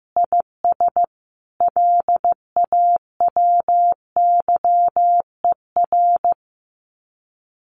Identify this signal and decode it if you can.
Morse code, heard as a keyed tone, about 15 wpm, 700 Hz. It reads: IS LAWYER
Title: Common QSO Phrases Encore 15wpm